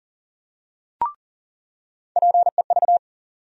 Morse code, heard as a keyed tone, about 40 wpm, 700 Hz. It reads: PEV
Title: USA Call Sign Postfixes - 3 Letters 40wpm